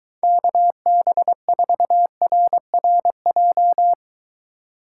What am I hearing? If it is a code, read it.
Morse code, heard as a keyed tone, about 23 wpm, 700 Hz. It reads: KB4RRJ